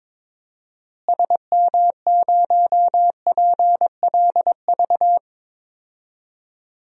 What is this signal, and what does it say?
Morse code, heard as a keyed tone, about 22 wpm, 700 Hz. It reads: SM0PLV
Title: DXCC-100 Call Signs - 6 Characters - Repeated 2x - 22wpm